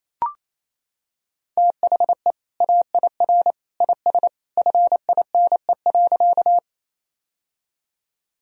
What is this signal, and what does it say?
Morse code, heard as a keyed tone, about 28 wpm, 700 Hz. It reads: THE AIR IS FINE.
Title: Sentences from Top 1000 Words - Review Part 1 - 28wpm